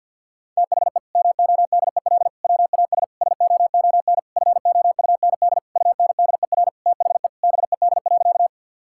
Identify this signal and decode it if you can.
Morse code, heard as a keyed tone, about 52 wpm, 700 Hz. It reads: THE MODEL WAS SOON FOUND UNDER THE BED.
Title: Sentences from Top 1000 Words - Review Part 5 - 50wpm